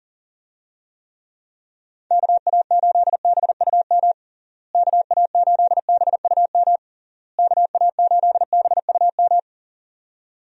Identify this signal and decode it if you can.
Morse code, heard as a keyed tone, about 40 wpm, 700 Hz. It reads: KA8BUM KA8BUM KA8BUM
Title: USA Call Signs - 6 Characters - Repeated 3x - 40wpm